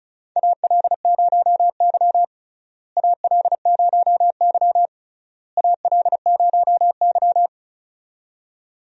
Morse code, heard as a keyed tone, about 36 wpm, 700 Hz. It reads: AL0Y AL0Y AL0Y
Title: USA Call Signs - 4 Characters - Repeated 3x - 35wpm